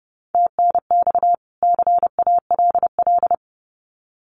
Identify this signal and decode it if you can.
Morse code, heard as a keyed tone, about 30 wpm, 700 Hz. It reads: TNX CALL
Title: Common QSO Phrases 30wpm